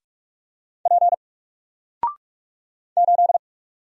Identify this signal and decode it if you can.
Morse code, heard as a keyed tone, about 44 wpm, 700 Hz. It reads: P 8
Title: Single Letter-Number 45wpm